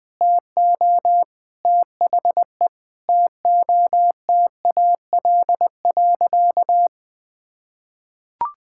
Morse code, heard as a keyed tone, about 21 wpm, 700 Hz. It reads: TO THE TOTAL.
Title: Sentences with 8-Word Phrases - 20wpm